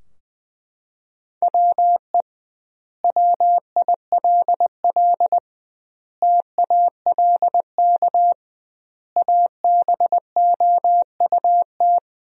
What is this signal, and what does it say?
Morse code, heard as a keyed tone, about 20 wpm, 700 Hz. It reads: WE WILL TALK ABOUT
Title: 2x Word-Spacing - Sentences from Top 400 Words - 20wpm